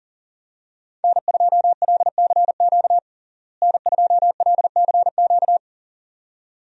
Morse code, heard as a keyed tone, about 40 wpm, 700 Hz. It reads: N2LCQ N2LCQ
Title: USA Call Signs - 5 Characters - Repeated 2x - 40wpm